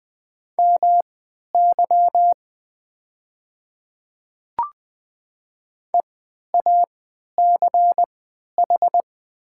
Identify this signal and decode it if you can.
Morse code, heard as a keyed tone, about 20 wpm, spaced as for 11 wpm, 700 Hz. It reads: MY EACH
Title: 3x Character Spacing - Top 100 Words - 20wpm